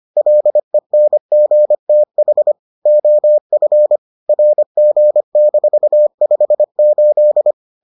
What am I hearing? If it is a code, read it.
Morse code, heard as a keyed tone, about 25 wpm, 600 Hz. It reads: LENGTH OF RG-58